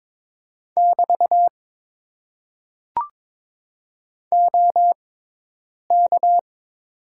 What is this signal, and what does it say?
Morse code, heard as a keyed tone, about 22 wpm, spaced as for 8 wpm, 700 Hz. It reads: <BT> OK
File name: Top 90 QSO Elements - 6x Character Spacing - 22wpm